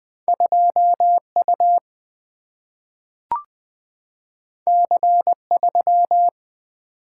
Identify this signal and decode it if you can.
Morse code, heard as a keyed tone, about 20 wpm, 700 Hz. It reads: 2U C3
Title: Two Letter-Number Combinations - 20wpm